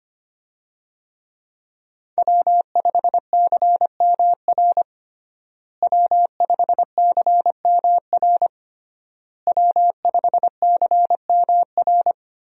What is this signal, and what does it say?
Morse code, heard as a keyed tone, about 25 wpm, 700 Hz. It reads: W5CMR W5CMR W5CMR
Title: USA Call Signs - 5 Characters - Repeated 3x - 25wpm